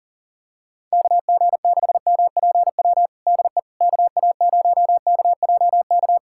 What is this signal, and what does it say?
Morse code, heard as a keyed tone, about 40 wpm, 700 Hz. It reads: KG6MPW DE KA0KJK
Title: USA Callsign DE Callsign - 6 Characters - 40wpm